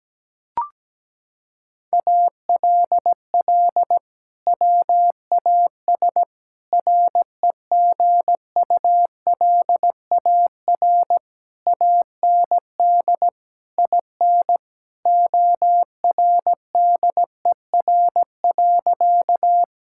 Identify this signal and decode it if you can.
Morse code, heard as a keyed tone, about 17 wpm, 700 Hz. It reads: ALL WAS REGULAR AND IN ORDER.
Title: Sentences from Top 1000 Words - Review Part 6 - 17wpm